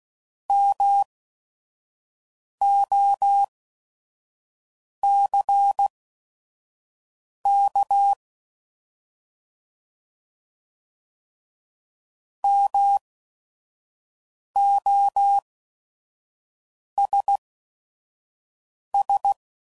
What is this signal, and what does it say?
Morse code, heard as a keyed tone, about 16 wpm, spaced as for 5 wpm, 800 Hz. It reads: MOCK MOSS